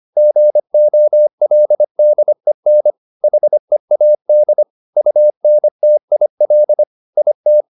Morse code, heard as a keyed tone, about 25 wpm, 600 Hz. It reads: GOLDEN HEAD UNTIL IT